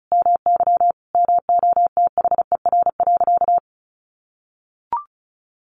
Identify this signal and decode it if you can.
Morse code, heard as a keyed tone, about 34 wpm, 700 Hz. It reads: MY MOTHER.
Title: Sentences from Top 600 Words 35wpm